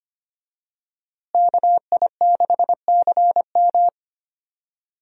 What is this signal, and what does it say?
Morse code, heard as a keyed tone, about 25 wpm, 700 Hz. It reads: KI6CM
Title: USA Call Signs - 5 Characters - Repeated 3x - 25wpm